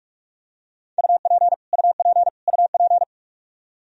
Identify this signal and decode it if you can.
Morse code, heard as a keyed tone, about 43 wpm, 700 Hz. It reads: UP UP UP
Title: Top 100 Words Repeated 45wpm